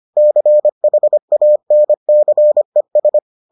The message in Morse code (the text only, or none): CHANCES